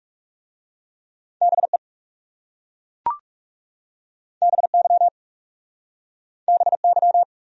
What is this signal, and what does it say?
Morse code, heard as a keyed tone, about 39 wpm, 700 Hz. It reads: BE BY BY